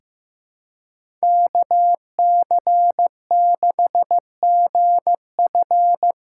Morse code, heard as a keyed tone, about 15 wpm, 700 Hz. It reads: KC6GF
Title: USA Call Signs - 6x Character Spacing - 15wpm